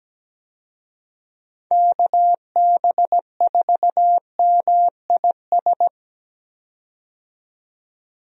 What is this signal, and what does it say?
Morse code, heard as a keyed tone, about 17 wpm, 700 Hz. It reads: KB4MIS
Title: USA Call Signs - 6 Characters - 17wpm